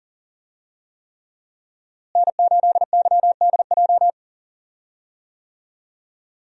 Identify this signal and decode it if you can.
Morse code, heard as a keyed tone, about 40 wpm, 700 Hz. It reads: N8YDJ
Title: USA Call Signs - 5 Characters - Repeated 3x - 40wpm